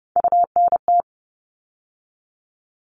Morse code, heard as a keyed tone, about 30 wpm, 700 Hz. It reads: UNT
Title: All Cut Numbers - 3-Digit 30wpm